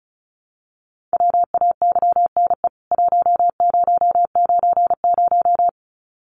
Morse code, heard as a keyed tone, about 35 wpm, 700 Hz. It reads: WAYNE 1090
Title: CWT Exchange 35wpm